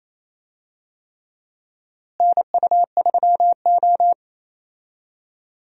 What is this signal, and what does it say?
Morse code, heard as a keyed tone, about 27 wpm, 700 Hz. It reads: NU3O
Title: USA Call Signs - 4 Characters - Repeated 3x - 28wpm